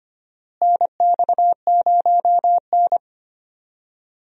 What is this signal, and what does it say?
Morse code, heard as a keyed tone, about 25 wpm, 700 Hz. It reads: NX0N
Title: USA Call Signs - 4 Characters - Repeated 2x - 25wpm